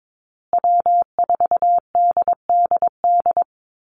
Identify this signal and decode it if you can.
Morse code, heard as a keyed tone, about 22 wpm, 700 Hz. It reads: W4DDD
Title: Call Signs Repeated 3x - 22wpm